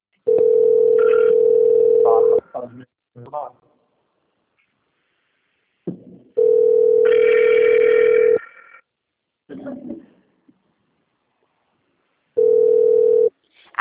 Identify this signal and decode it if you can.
Ringback tone (North America), 440 + 480 Hz, with speech.
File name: out- - - - - .25128